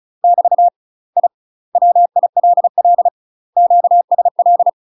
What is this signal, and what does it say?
Morse code, heard as a keyed tone, about 36 wpm, 700 Hz. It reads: <BT> I WILL QSL
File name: UK MASTER QSOs TRAINING/35WPM